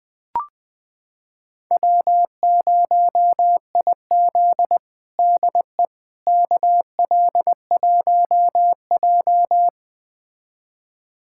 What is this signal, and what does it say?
Morse code, heard as a keyed tone, about 21 wpm, 700 Hz. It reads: W0IZ DE KL1J
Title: USA Callsign DE Callsign - 4 Characters - 20wpm